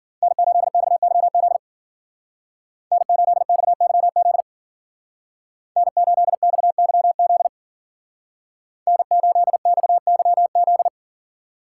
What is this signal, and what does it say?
Morse code, heard as a keyed tone, about 52 wpm, 700 Hz. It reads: N8XYZ N8XYZ N8XYZ N8XYZ